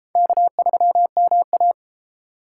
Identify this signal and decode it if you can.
Morse code, heard as a keyed tone, about 32 wpm, 700 Hz. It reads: K3MA